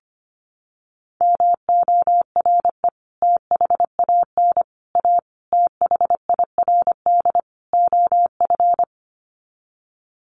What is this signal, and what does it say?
Morse code, heard as a keyed tone, about 25 wpm, 700 Hz. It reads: MORE THAN A THIRD OF
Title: Sets of 5 Words Encore2 25wpm